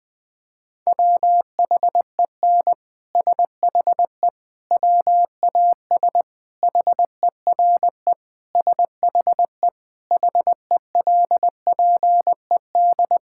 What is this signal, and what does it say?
Morse code, heard as a keyed tone, about 20 wpm, 700 Hz. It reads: WHEN SHE WAS HERE SHE HELPED
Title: Sentences with 3-Word Phrases - 20wpm